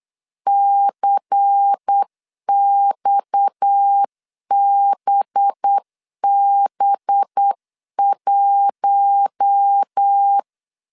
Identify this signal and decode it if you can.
Morse code, heard as a keyed tone, about 8 wpm, 800 Hz. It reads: CXBB1
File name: CW / DET / 1 6 1500C S C 8wpm 5219